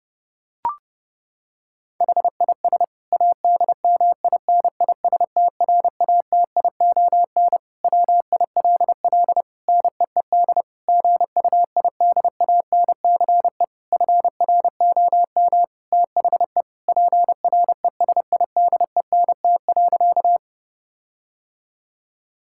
Morse code, heard as a keyed tone, about 30 wpm, 700 Hz. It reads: HIS ADMINISTRATION WILL NEED GUIDANCE FROM THE PRESIDENT.
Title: Sentences from Top 1600 Words 30wpm